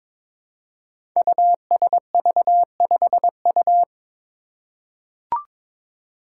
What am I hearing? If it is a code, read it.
Morse code, heard as a keyed tone, about 22 wpm, 700 Hz. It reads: USV5U
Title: Five Challenging Letter-Number Combinations 22wpm